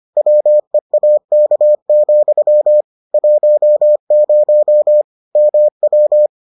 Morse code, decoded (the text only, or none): WEAK, 10 MW